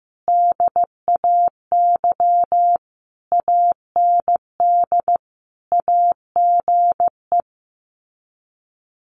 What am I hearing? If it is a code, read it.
Morse code, heard as a keyed tone, about 15 wpm, 700 Hz. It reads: DAY AND AGE
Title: Sets of 5 Words Encore 15wpm